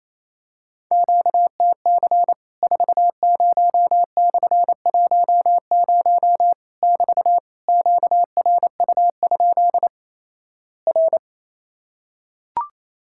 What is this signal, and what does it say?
Morse code, heard as a keyed tone, about 28 wpm, 700 Hz. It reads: QTC 40/10 <BT> QRU? R
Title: Worked All Europe DX Contest - QTC Exchange - 28wpm